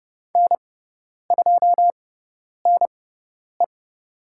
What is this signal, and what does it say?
Morse code, heard as a keyed tone, about 30 wpm, spaced as for 10 wpm, 700 Hz. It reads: N2NE
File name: Lesson 17g - Letter u - Call Signs 30wpm